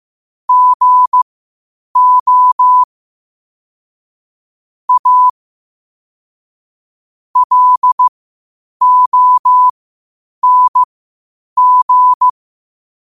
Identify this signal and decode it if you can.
Morse code, heard as a keyed tone, about 15 wpm, spaced as for 8 wpm, 1000 Hz. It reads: GO A LONG
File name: qotdfiles / 8wpm